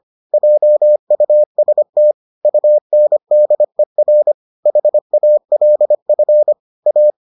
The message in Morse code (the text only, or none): JUST UNDER HALF A